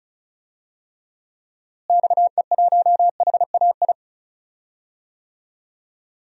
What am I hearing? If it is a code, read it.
Morse code, heard as a keyed tone, about 36 wpm, 700 Hz. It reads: XE1HAI
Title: DXCC-100 Call Signs - 6 Characters - Repeated 3x - 35wpm